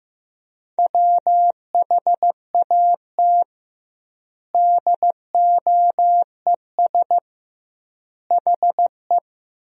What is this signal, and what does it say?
Morse code, heard as a keyed tone, about 15 wpm, 700 Hz. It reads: WHAT DOES HE
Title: 2x Word-Spacing - Sentences from Top 200 Words - 15wpm